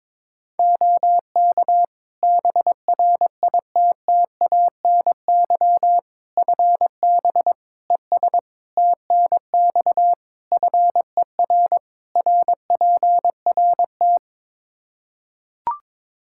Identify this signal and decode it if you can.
Morse code, heard as a keyed tone, about 22 wpm, 700 Hz. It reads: OK BRITTANY FB ES TNX FER RPRT
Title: Practice QSO - Exchange 2 - Info - 22wpm